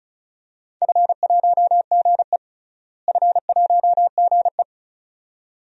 Morse code, heard as a keyed tone, about 36 wpm, 700 Hz. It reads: F1GE F1GE
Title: DXCC-100 Call Signs - 4 Characters - Repeated 2x - 35wpm